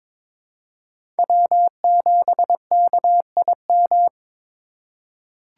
Morse code, heard as a keyed tone, about 22 wpm, 700 Hz. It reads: W7KIM
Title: USA Call Signs - 5 Characters - 2x Character Spacing - 22wpm